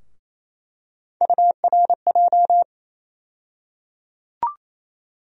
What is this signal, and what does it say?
Morse code, heard as a keyed tone, about 28 wpm, 700 Hz. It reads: URJ
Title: USA Call Sign Postfixes - 3 Letters 28wpm